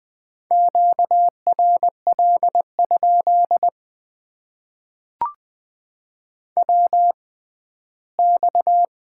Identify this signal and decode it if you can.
Morse code, heard as a keyed tone, about 20 wpm, spaced as for 14 wpm, 700 Hz. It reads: QRL? WX
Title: Top 90 QSO Elements - 6x Character Spacing - 20wpm